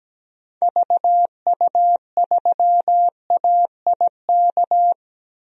Morse code, heard as a keyed tone, about 17 wpm, 700 Hz. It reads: VU3AIK
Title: DXCC-100 Call Signs - 6 Characters - 17wpm